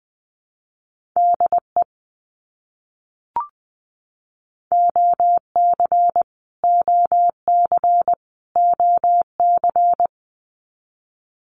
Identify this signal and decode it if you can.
Morse code, heard as a keyed tone, about 20 wpm, 700 Hz. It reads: DE OC OC OC
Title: Abbreviations Repeated 3x - 20wpm